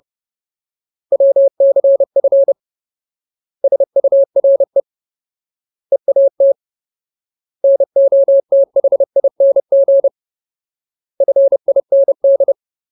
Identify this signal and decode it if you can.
Morse code, heard as a keyed tone, about 30 wpm, 550 Hz. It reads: WCF SURE EAT NOTHING FIND
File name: Ultimate Word List / 30WPM Ultimate